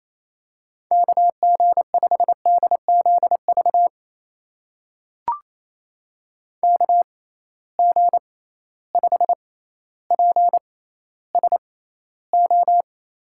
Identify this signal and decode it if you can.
Morse code, heard as a keyed tone, about 28 wpm, spaced as for 14 wpm, 700 Hz. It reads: KG5DZV KG5PSO
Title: USA Call Signs - 6 Characters - 6x Character Spacing - 28wpm